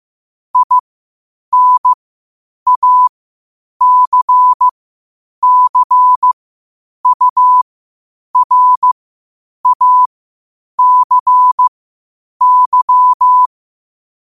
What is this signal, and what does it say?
Morse code, heard as a keyed tone, about 15 wpm, spaced as for 8 wpm, 1000 Hz. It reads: INACCURACY
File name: qotdfiles / 8wpm